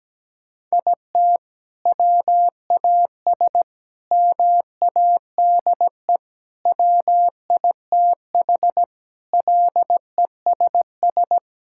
Morse code, heard as a keyed tone, about 17 wpm, 700 Hz. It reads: IT WAS MADE WITH LESS
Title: Sentences from Top 1000 Words - Review Part 5 - 17wpm